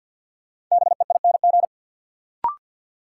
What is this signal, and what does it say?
Morse code, heard as a keyed tone, about 53 wpm, 700 Hz. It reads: BEING
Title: Sentences from Top 600 Words - New Word Drill - 50wpm